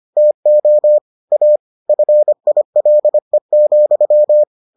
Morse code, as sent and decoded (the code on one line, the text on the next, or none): - --- / .- / ..-. .. .-.. --..--
TO A FILE,